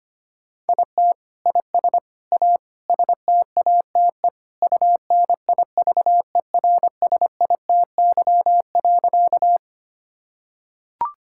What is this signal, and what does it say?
Morse code, heard as a keyed tone, about 25 wpm, 700 Hz. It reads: IT IS A STATE UNIVERSITY.
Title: Sentences from Top 1000 Words - Review Part 5 - 25wpm